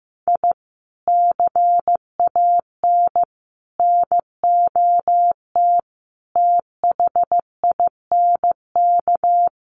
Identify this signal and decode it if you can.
Morse code, heard as a keyed tone, about 15 wpm, 700 Hz. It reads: I CAN NOT THINK
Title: Sets of 5 Words 15wpm